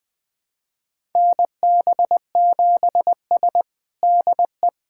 Morse code, heard as a keyed tone, about 21 wpm, 700 Hz. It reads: NB7S DE
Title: USA Callsign DE Callsign - 4 Characters - 20wpm